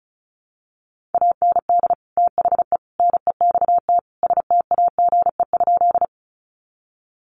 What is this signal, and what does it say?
Morse code, heard as a keyed tone, about 34 wpm, 700 Hz. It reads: AND THE NEXT STAGE?
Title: Sentences from Top 1200 Words 35wpm